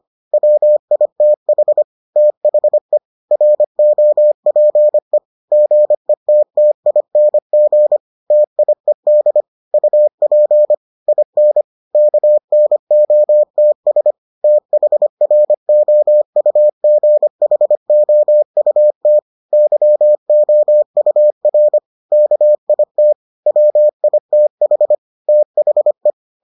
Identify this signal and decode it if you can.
Morse code, heard as a keyed tone, about 25 wpm, 600 Hz. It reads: WITH THE ROPE GETTING TIED UP IN KNOTS THROUGHOUT YOUR KIT WITH THE